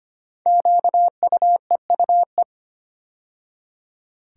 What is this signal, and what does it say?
Morse code, heard as a keyed tone, about 25 wpm, 700 Hz. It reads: QUEUE
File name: 5 Letter Words - First 3 Letters Given - 25wpm